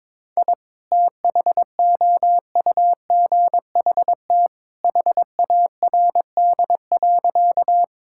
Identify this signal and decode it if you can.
Morse code, heard as a keyed tone, about 22 wpm, 700 Hz. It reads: I THOUGHT HARD.
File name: practice / 22wpm / Sentences with 2-Word Phrases - 22wpm